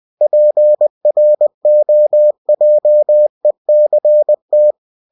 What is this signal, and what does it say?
Morse code, heard as a keyed tone, about 20 wpm, 600 Hz. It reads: PROJECT
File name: cw-books / winnie-the-pooh / winnie-20wpm-page-0120